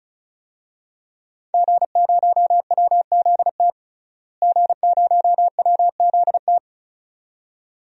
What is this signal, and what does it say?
Morse code, heard as a keyed tone, about 34 wpm, 700 Hz. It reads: G0WZT G0WZT
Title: DXCC-100 Call Signs - 5 Characters - Repeated 2x - 35wpm